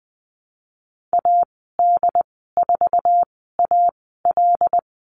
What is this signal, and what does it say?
Morse code, heard as a keyed tone, about 20 wpm, spaced as for 15 wpm, 700 Hz. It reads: AD4AL
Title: 076 Lesson 13g - Letter L - Call Signs 20wpm